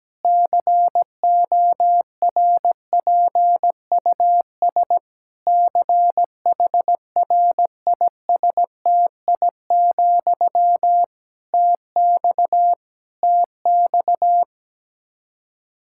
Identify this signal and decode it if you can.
Morse code, heard as a keyed tone, about 17 wpm, 700 Hz. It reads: CORPUS CHRISTI, TX TX
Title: K1USN SST QSO Practice 17wpm